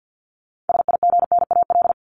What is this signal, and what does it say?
Morse code, heard as a keyed tone, about 50 wpm, 700 Hz. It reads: SIGNAL
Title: Top 1251-1500 Words 50wpm